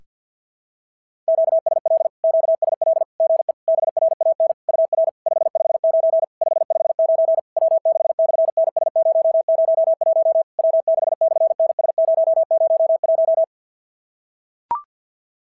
Morse code, heard as a keyed tone, about 46 wpm, 660 Hz. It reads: QSL QSL GE BRAN UR 559 559 W6/NS001 W6/NS001
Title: SOTA QSO Practice 50wpm